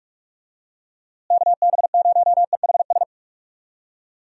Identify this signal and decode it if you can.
Morse code, heard as a keyed tone, about 46 wpm, 700 Hz. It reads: KB0EHS
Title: USA Call Signs - 6 Characters - Repeated 2x - 45wpm